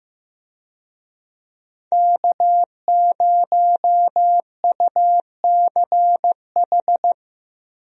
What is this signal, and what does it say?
Morse code, heard as a keyed tone, about 15 wpm, 700 Hz. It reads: K0UCH
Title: USA Call Signs - 5 Characters - Repeated 3x - 15wpm